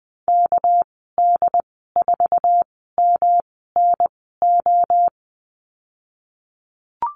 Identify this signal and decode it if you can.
Morse code, heard as a keyed tone, about 19 wpm, spaced as for 14 wpm, 700 Hz. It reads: KD4MNO
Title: 207 Lesson 33g - Letter j - Call Signs 20wpm